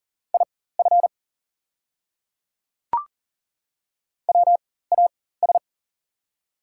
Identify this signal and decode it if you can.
Morse code, heard as a keyed tone, about 39 wpm, spaced as for 19 wpm, 700 Hz. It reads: IF WAS
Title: 4x Character Spacing - Top 100 Words - 40wpm